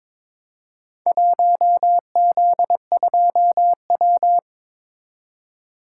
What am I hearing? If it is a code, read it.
Morse code, heard as a keyed tone, about 22 wpm, 700 Hz. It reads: 1Z2W
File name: practice / 22wpm / Four Letter-Number Combinations - 22wpm